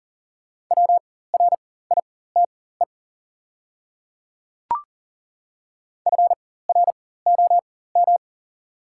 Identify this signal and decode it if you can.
Morse code, heard as a keyed tone, about 40 wpm, spaced as for 19 wpm, 700 Hz. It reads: WRITE FROM
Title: 4x Character Spacing - Top 100 Words - 40wpm